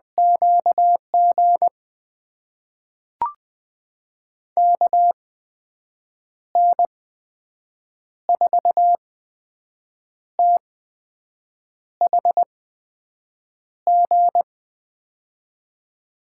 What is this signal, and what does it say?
Morse code, heard as a keyed tone, about 20 wpm, spaced as for 6 wpm, 700 Hz. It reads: QG KN4THG
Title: USA Call Signs - 6 Characters - 8x Character Spacing - 20wpm